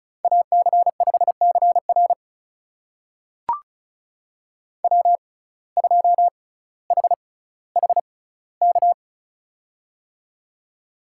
Morse code, heard as a keyed tone, about 34 wpm, spaced as for 18 wpm, 700 Hz. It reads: AC5CR W2HHK
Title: USA Call Signs - 5 Characters - 6x Character Spacing - 35wpm